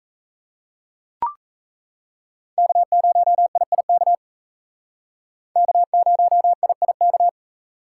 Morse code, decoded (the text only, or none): K0IIK K0IIK